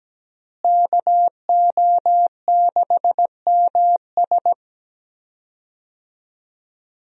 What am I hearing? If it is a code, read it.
Morse code, heard as a keyed tone, about 17 wpm, 700 Hz. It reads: KO6MS
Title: USA Call Signs - 5 Characters - Repeated 2x - 17wpm